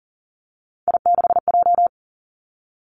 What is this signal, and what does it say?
Morse code, heard as a keyed tone, about 41 wpm, 700 Hz. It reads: I6J
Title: Three Challenging Letter-Number Combinations 40wpm